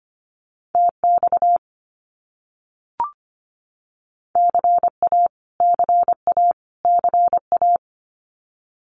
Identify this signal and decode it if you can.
Morse code, heard as a keyed tone, about 25 wpm, 700 Hz. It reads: TX CA CA CA
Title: US States Repeated 3x 25wpm